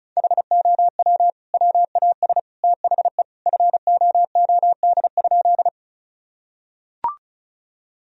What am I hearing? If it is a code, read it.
Morse code, heard as a keyed tone, about 36 wpm, 700 Hz. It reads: HOW WAS THE FOOD?
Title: Sentences from Top 1000 Words - Review Part 1 - 35wpm